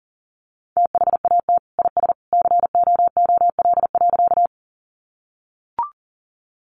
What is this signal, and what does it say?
Morse code, heard as a keyed tone, about 37 wpm, 700 Hz. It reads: THAT IS COOL.
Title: Sentences from Top 1200 Words 40wpm